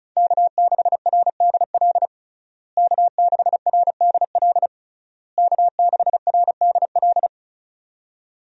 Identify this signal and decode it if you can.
Morse code, heard as a keyed tone, about 36 wpm, 700 Hz. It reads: K6RDL K6RDL K6RDL
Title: USA Call Signs - 5 Characters - Repeated 3x - 35wpm